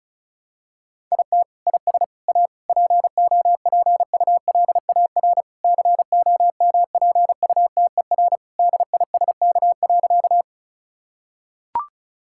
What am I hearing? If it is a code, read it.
Morse code, heard as a keyed tone, about 34 wpm, 700 Hz. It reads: IT IS A POPULAR COMPUTER DISK.
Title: Sentences from Top 1000 Words - Review Part 6 - 35wpm